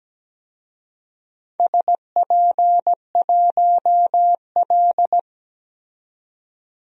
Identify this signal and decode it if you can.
Morse code, heard as a keyed tone, about 17 wpm, 700 Hz. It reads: SP1L
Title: DXCC-100 Call Signs - 4 Characters - 17wpm